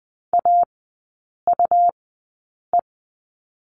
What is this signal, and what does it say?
Morse code, heard as a keyed tone, about 20 wpm, spaced as for 8 wpm, 700 Hz. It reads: AUE
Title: 127 Lesson 21e - Number 6 - Three Characters 20wpm